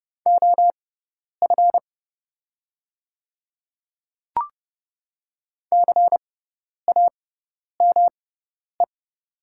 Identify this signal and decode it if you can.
Morse code, heard as a keyed tone, about 29 wpm, spaced as for 10 wpm, 690 Hz. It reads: OF CAME
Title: 6x Character Spacing - Top 200 Words - 30wpm